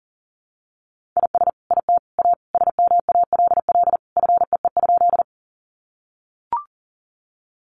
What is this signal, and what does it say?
Morse code, heard as a keyed tone, about 40 wpm, 700 Hz. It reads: IS IT A SMALL FEE?
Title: Sentences from Top 1500 Words 40wpm